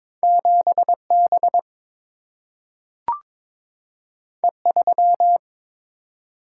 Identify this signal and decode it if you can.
Morse code, heard as a keyed tone, about 22 wpm, 700 Hz. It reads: 7B E3
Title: Single-Letter Single-Number 22wpm